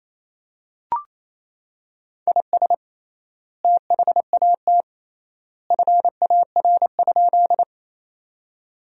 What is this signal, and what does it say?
Morse code, heard as a keyed tone, about 28 wpm, 700 Hz. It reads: IS THAT FAR?
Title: 3x Word-Spacing - Sentences from Top 300 Words - 28wpm